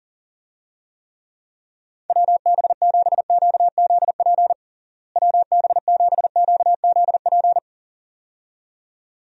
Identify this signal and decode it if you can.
Morse code, heard as a keyed tone, about 37 wpm, 700 Hz. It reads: WB7QZP WB7QZP
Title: USA Call Signs - 6 Characters - Repeated 2x - 40wpm